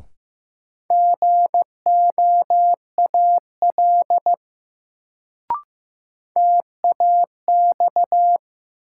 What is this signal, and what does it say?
Morse code, heard as a keyed tone, about 15 wpm, 700 Hz. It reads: GOAL TAX
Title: Sentences from Top 1000 Words - Review Part 5 - New Word Drill - 15wpm